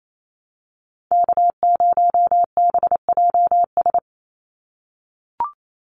Morse code, heard as a keyed tone, about 28 wpm, 700 Hz. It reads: K0BJS
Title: USA Call Signs 28wpm